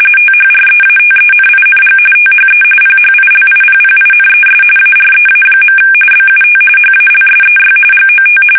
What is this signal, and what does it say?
RTTY, about 45.45 baud. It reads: THREATENING THE PHILIPPINES AREA. BUT HALSEY GOT
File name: fsk850 / battleshipnj